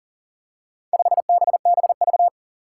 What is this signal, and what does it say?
Morse code, heard as a keyed tone, about 40 wpm, 700 Hz. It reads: 5BBV
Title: Four Challenging Letter-Number Combinations 40wpm